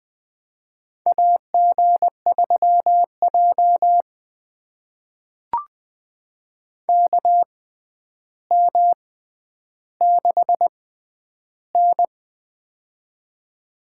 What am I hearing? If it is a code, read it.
Morse code, heard as a keyed tone, about 21 wpm, spaced as for 10 wpm, 700 Hz. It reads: AG3J KM6N
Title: USA Call Signs - 4 Characters - 6x Character Spacing - 20wpm